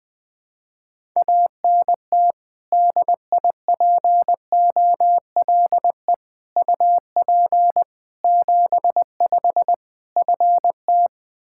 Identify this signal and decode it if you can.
Morse code, heard as a keyed tone, about 20 wpm, 700 Hz. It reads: ANT DIPOLE UP 75 FT
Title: K1USN SST QSO Practice 20wpm